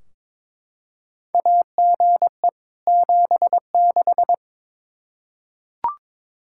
Morse code, heard as a keyed tone, about 22 wpm, 700 Hz. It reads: AGE 76
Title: Practice QSO - Exchange 3 - More Info - 22wpm